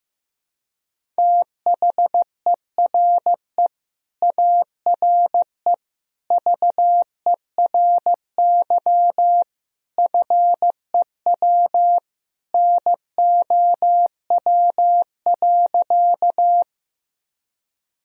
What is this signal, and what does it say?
Morse code, heard as a keyed tone, about 15 wpm, 700 Hz. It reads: THERE ARE VERY FEW NOW.
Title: Sentences from Top 1000 Words - Review Part 1 - 15wpm